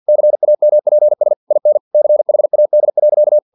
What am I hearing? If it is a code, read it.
Morse code, heard as a keyed tone, about 50 wpm, 600 Hz. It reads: CAMPS IN CHAD.
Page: Never 49wpm